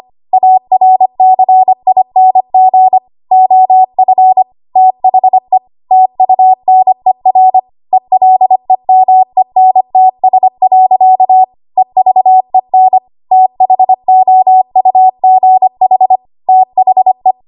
Morse code, heard as a keyed tone, about 25 wpm, 750 Hz. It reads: ARCING OF THE TUNER ELEMENTS. EVEN THOUGH THE
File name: Archive / 25 WPM / 25WPM